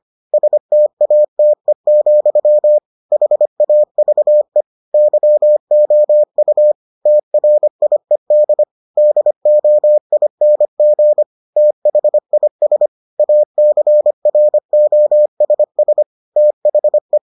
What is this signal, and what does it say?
Morse code, heard as a keyed tone, about 25 wpm, 600 Hz. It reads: STATE, HAVE YOU TRIED DOING THIS ACROSS THE